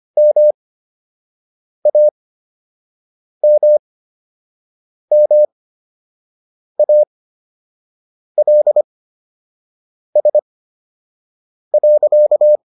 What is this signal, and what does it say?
Morse code, heard as a keyed tone, about 25 wpm, spaced as for 6 wpm, 600 Hz. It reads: MAMMALS.